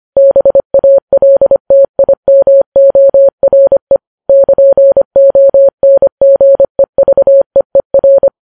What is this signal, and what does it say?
Morse code, heard as a keyed tone, about 25 wpm, 550 Hz. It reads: BALTIMORE <KN>ONGEVEER